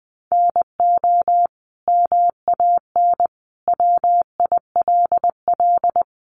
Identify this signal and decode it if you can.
Morse code, heard as a keyed tone, about 20 wpm, 700 Hz. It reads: NO MAN WILL
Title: Sentences from Top 1400 Words 20wpm